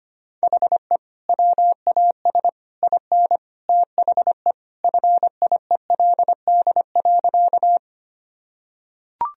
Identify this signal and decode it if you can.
Morse code, heard as a keyed tone, about 25 wpm, 700 Hz. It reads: HE WAS IN THE FIELD.
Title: Sentences from Top 1000 Words - Review Part 2 - 25wpm